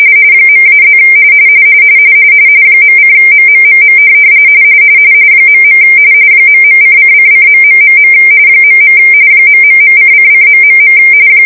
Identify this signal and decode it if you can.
RTTY, about 45.45 baud. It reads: .-+ .:MMMM .. ..::::.... .. ...:::...